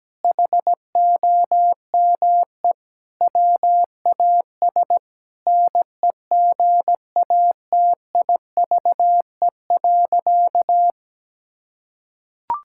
Morse code, heard as a keyed tone, about 17 wpm, 700 Hz. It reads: HOME WAS NEGATIVE.
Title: Sentences from Top 1000 Words - Review Part 6 - 17wpm